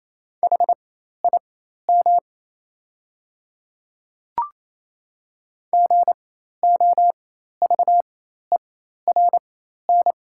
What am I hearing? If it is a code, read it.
Morse code, heard as a keyed tone, about 28 wpm, spaced as for 13 wpm, 700 Hz. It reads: HIM GOVERN
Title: 4x Character Spacing - Top 500 Words - 28wpm